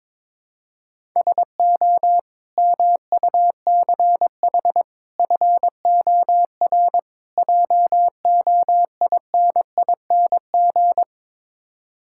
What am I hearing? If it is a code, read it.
Morse code, heard as a keyed tone, about 22 wpm, 700 Hz. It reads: SO MUCH FOR JOINING
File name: practice / 22wpm / Build A Phrase - 5-Word Phrases - 22wpm